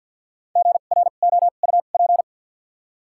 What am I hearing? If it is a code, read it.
Morse code, heard as a keyed tone, about 48 wpm, spaced as for 36 wpm, 700 Hz. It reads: GROUP
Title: 2x Character Spacing - Top 500 Words - 50wpm